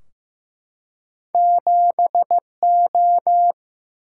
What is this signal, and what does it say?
Morse code, heard as a keyed tone, about 15 wpm, 700 Hz. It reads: 7O
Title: Single-Letter Single-Number 15wpm